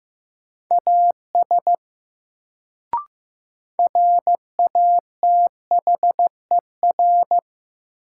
Morse code, heard as a keyed tone, about 15 wpm, 700 Hz. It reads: AS RATHER
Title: Sets of 2 Words - New Word Drill - 15wpm